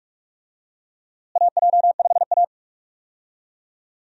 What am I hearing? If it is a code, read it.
Morse code, heard as a keyed tone, about 42 wpm, 700 Hz. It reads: AJ5A
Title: USA Call Signs - 4 Characters - Repeated 2x - 45wpm